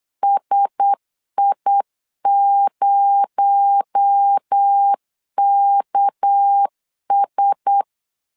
Morse code, heard as a keyed tone, about 8 wpm, 800 Hz. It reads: SI0KS
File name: 8wpm